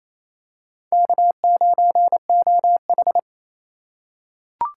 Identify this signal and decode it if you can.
Morse code, heard as a keyed tone, about 28 wpm, 700 Hz. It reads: K9OH
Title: USA Call Signs - 4 Characters - 4x Character Spacing - 28wpm